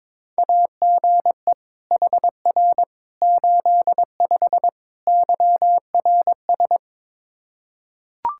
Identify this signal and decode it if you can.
Morse code, heard as a keyed tone, about 22 wpm, 700 Hz. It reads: AGE HR 85 YRS
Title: Practice QSO - Exchange 3 - More Info - 22wpm